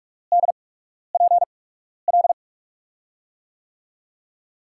Morse code, heard as a keyed tone, about 46 wpm, spaced as for 12 wpm, 700 Hz. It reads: DPL
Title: USA Call Signs - 5 Characters - 8x Character Spacing - 45wpm